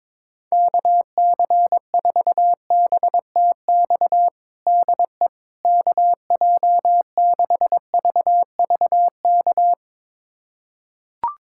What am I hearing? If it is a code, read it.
Morse code, heard as a keyed tone, about 22 wpm, 700 Hz. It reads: KC4BTX DE KJ6VVK
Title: USA Callsign DE Callsign - 6 Characters - 22wpm